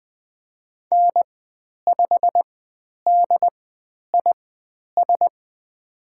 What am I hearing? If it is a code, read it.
Morse code, heard as a keyed tone, about 20 wpm, spaced as for 10 wpm, 700 Hz. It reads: N5DIS
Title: 109 Lesson 18g - Letter m - Call Signs 20wpm